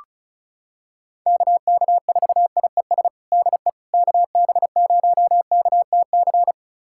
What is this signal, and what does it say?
Morse code, heard as a keyed tone, about 34 wpm, 700 Hz. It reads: KK4IES DE KB0KTC
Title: USA Callsign DE Callsign 35wpm